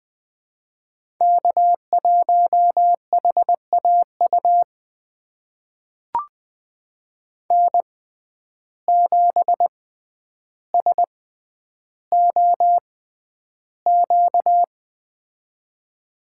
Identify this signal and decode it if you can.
Morse code, heard as a keyed tone, about 21 wpm, spaced as for 10 wpm, 700 Hz. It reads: K1HAU N7SOQ
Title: USA Call Signs - 5 Characters - 6x Character Spacing - 20wpm